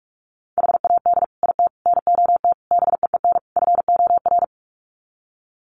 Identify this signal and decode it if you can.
Morse code, heard as a keyed tone, about 44 wpm, 700 Hz. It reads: HAD IT NOT BEEN FOR
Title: Sets of 5 Words Encore2 45wpm